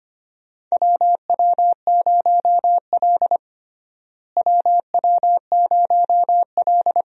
Morse code, heard as a keyed tone, about 25 wpm, 700 Hz. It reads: WW0L WW0L
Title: USA Call Signs - 4 Characters - Repeated 2x - 25wpm